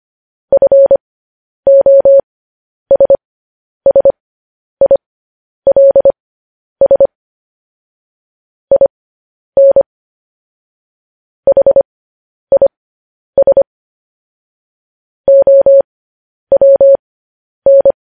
Morse code, heard as a keyed tone, about 25 wpm, spaced as for 10 wpm, 550 Hz. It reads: FOSSILS IN HIS OWN
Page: news headlines in morse code at 10 wpm